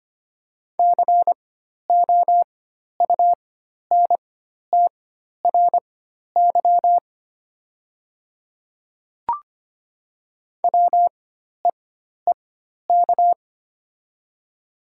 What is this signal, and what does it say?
Morse code, heard as a keyed tone, about 25 wpm, spaced as for 12 wpm, 700 Hz. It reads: COUNTRY WEEK
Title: 4x Character Spacing - Top 500 Words - 25wpm